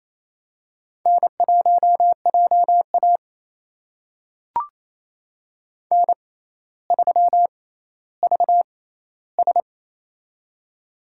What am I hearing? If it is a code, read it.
Morse code, heard as a keyed tone, about 28 wpm, spaced as for 14 wpm, 700 Hz. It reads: N1JA N3VS
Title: USA Call Signs - 4 Characters - 6x Character Spacing - 28wpm